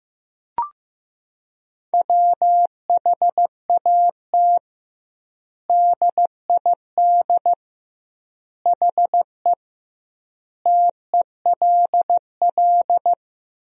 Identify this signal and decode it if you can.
Morse code, heard as a keyed tone, about 15 wpm, 700 Hz. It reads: WHAT DID HE TELL
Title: 2x Word-Spacing - Sentences from Top 200 Words - 15wpm